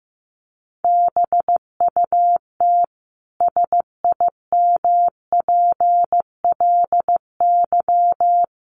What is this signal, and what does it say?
Morse code, heard as a keyed tone, about 15 wpm, 700 Hz. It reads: BUT SIMPLY
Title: Sentences from Top 1300 Words 15wpm